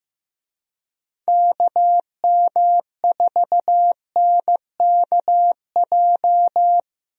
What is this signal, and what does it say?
Morse code, heard as a keyed tone, about 15 wpm, 700 Hz. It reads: KM4NKJ
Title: USA Call Signs - 6 Characters - Repeated 2x - 15wpm